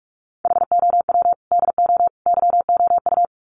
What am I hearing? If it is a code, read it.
Morse code, heard as a keyed tone, about 44 wpm, 700 Hz. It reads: HOW DO YOU
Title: Sets of 3 Words 45wpm